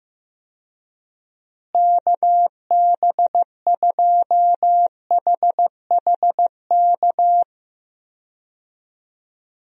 Morse code, heard as a keyed tone, about 15 wpm, 700 Hz. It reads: KB2HHK
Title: USA Call Signs - 6 Characters - Repeated 2x - 15wpm